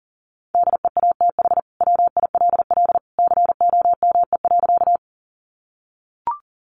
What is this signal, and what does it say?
Morse code, heard as a keyed tone, about 40 wpm, 700 Hz. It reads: DEATH WILL COME.
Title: Sentences from Top 1100 Words 40wpm